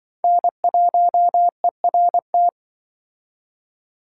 Morse code, heard as a keyed tone, about 25 wpm, 700 Hz. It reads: N1ERT
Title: Speed Racing - USA Call Signs 20wpm